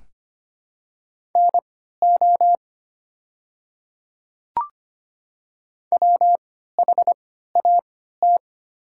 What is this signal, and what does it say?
Morse code, heard as a keyed tone, about 25 wpm, spaced as for 14 wpm, 690 Hz. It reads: NO WHAT
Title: 3x Character Spacing - Top 100 Words - 25wpm